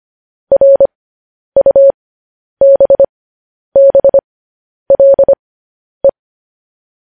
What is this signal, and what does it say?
Morse code, heard as a keyed tone, about 25 wpm, spaced as for 10 wpm, 550 Hz. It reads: RUBBLE
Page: news headlines in morse code at 10 wpm